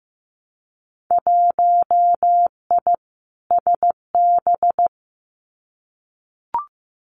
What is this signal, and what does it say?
Morse code, heard as a keyed tone, about 15 wpm, 700 Hz. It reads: 1I SB
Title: Winter Field Day Exchanges Encore 15wpm